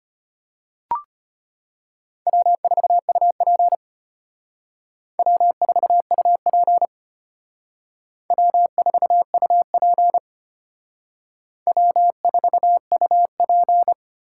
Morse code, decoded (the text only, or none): W4UP W4UP W4UP W4UP